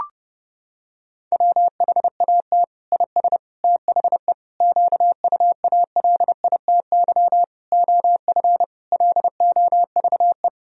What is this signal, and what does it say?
Morse code, heard as a keyed tone, about 30 wpm, 700 Hz. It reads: WHAT IS THE QUALITY OF LOVE
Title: Sentences from Top 1000 Words - Review Part 4 - 30wpm